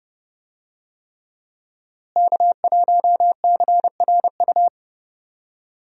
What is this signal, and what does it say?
Morse code, heard as a keyed tone, about 30 wpm, 700 Hz. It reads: K1CRU
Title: USA Call Signs - 5 Characters - Repeated 3x - 30wpm